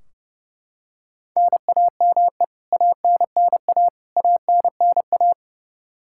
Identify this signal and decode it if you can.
Morse code, heard as a keyed tone, about 32 wpm, 700 Hz. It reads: NAME ANNA ANNA
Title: Practice QSO - Exchange 1 - Beginning - 30wpm